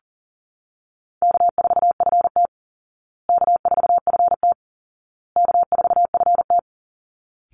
Morse code, heard as a keyed tone, about 40 wpm, 700 Hz. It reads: K4FT K4FT K4FT
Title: Call Signs Repeated 2x - 40wpm